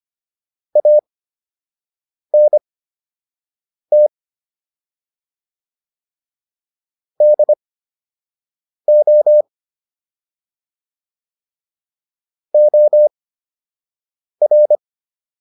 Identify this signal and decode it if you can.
Morse code, heard as a keyed tone, about 25 wpm, spaced as for 6 wpm, 600 Hz. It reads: ANT DO OR